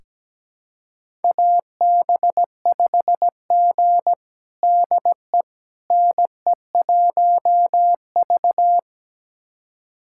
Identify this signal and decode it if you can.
Morse code, heard as a keyed tone, about 17 wpm, 700 Hz. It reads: AB5G DE NE1V
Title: USA Callsign DE Callsign - 4 Characters - 17wpm